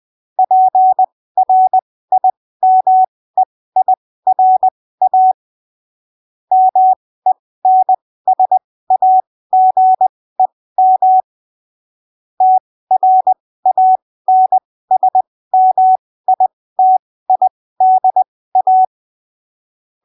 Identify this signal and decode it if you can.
Morse code, heard as a keyed tone, about 20 wpm, spaced as for 15 wpm, 750 Hz. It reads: PRIMEIRA MENSAGEM TRANSMITIDA
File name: arquivos / texto / texto 17 20ppm